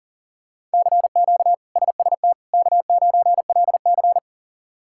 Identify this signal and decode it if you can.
Morse code, heard as a keyed tone, about 40 wpm, 700 Hz. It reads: CQ SST K9LC
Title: K1USN SST QSO Practice - 40wpm